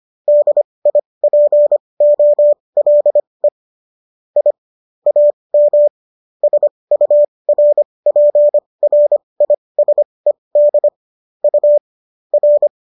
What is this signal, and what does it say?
Morse code, heard as a keyed tone, about 25 wpm, spaced as for 20 wpm, 590 Hz. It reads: DIPOLE I AM SURPRISED U R